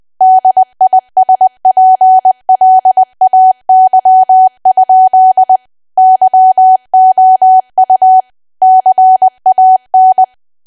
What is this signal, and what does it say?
Morse code, heard as a keyed tone, about 20 wpm, 750 Hz. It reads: DISPLAY? YOU CAN
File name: cwa / W1AW-Level3 / 20WPM